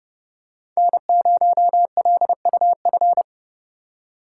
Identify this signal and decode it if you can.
Morse code, heard as a keyed tone, about 31 wpm, 700 Hz. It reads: N0LUF